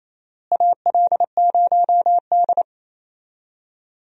Morse code, heard as a keyed tone, about 28 wpm, 700 Hz. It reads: AL0D
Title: USA Call Signs - 4 Characters - Repeated 2x - 28wpm